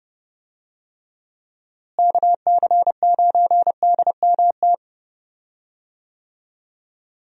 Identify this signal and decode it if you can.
Morse code, heard as a keyed tone, about 30 wpm, 700 Hz. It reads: KC9DMT
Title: USA Call Signs - 6 Characters - Repeated 2x - 30wpm